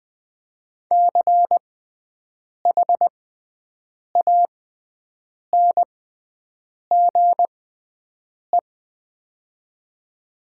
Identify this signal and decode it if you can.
Morse code, heard as a keyed tone, about 20 wpm, spaced as for 7 wpm, 700 Hz. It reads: CHANGE
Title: 6x Character Spacing - Top 200 Words - 20wpm